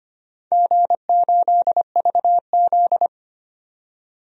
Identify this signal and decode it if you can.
Morse code, heard as a keyed tone, about 25 wpm, 700 Hz. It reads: G8VZ
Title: DXCC-100 Call Signs - 4 Characters - 25wpm